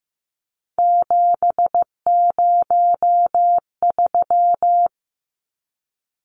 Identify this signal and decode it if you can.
Morse code, heard as a keyed tone, about 15 wpm, 700 Hz. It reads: 703
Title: Numbers - 3-Digit 15wpm